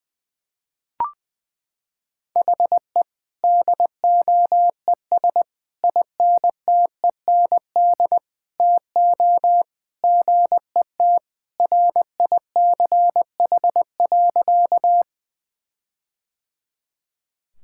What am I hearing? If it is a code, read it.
Morse code, heard as a keyed tone, about 20 wpm, 700 Hz. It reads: HE DOES INTEND TO GET RICH.
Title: Sentences from Top 1600 Words 20wpm